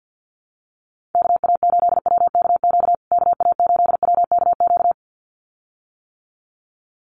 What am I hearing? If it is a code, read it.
Morse code, heard as a keyed tone, about 50 wpm, 700 Hz. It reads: KA8WKQ KA8WKQ
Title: USA Call Signs Repeated 2x 50wpm